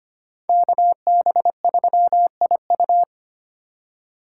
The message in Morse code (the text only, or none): KB3IU